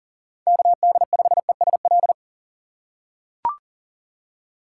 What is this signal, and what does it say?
Morse code, heard as a keyed tone, about 40 wpm, 700 Hz. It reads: KD5ESL
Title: USA Call Signs - 6 Characters - 6x Character Spacing - 40wpm